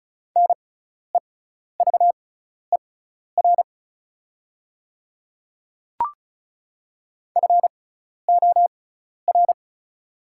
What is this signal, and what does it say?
Morse code, heard as a keyed tone, about 34 wpm, spaced as for 12 wpm, 690 Hz. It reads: NEVER FOR
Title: 6x Character Spacing - Top 300 Words - 35wpm